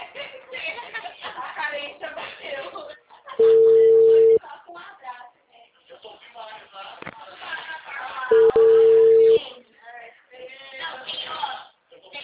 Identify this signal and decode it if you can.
Ringback tone (China, 450 Hz), with speech.
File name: out- -2012- - - .16183